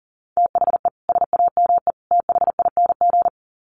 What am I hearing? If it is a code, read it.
Morse code, heard as a keyed tone, about 40 wpm, 700 Hz. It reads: THE SAME THING
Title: Sets of 3 Words 40wpm